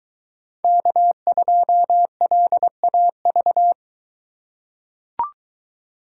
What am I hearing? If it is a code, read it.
Morse code, heard as a keyed tone, about 23 wpm, 700 Hz. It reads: K2LAV